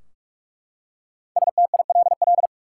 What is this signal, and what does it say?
Morse code, heard as a keyed tone, about 43 wpm, 700 Hz. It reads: STILL
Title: Five Letter Words - 45wpm